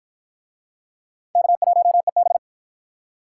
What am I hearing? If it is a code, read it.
Morse code, heard as a keyed tone, about 51 wpm, 700 Hz. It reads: K1EB